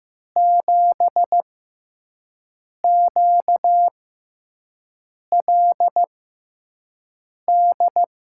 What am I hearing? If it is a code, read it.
Morse code, heard as a keyed tone, about 15 wpm, spaced as for 5 wpm, 700 Hz. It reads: 7QLD
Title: USA Call Signs - 6 Characters - 6x Character Spacing - 15wpm